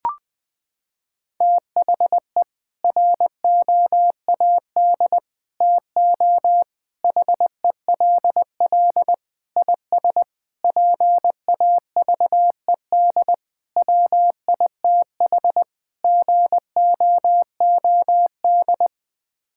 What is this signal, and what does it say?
Morse code, heard as a keyed tone, about 20 wpm, 700 Hz. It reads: THE ROAD TO HELL IS PAVED WITH GOOD
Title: Sets of 8 Words - 20wpm